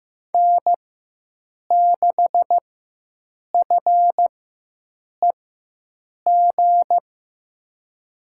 Morse code, heard as a keyed tone, about 15 wpm, spaced as for 7 wpm, 700 Hz. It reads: N6FEG
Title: USA Call Signs - 5 Characters - 4x Character Spacing - 15wpm